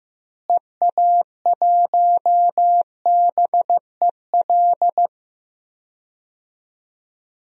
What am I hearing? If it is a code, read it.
Morse code, heard as a keyed tone, about 15 wpm, 700 Hz. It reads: EA1BEL
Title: DXCC-100 Call Signs - 6 Characters - Repeated 2x - 15wpm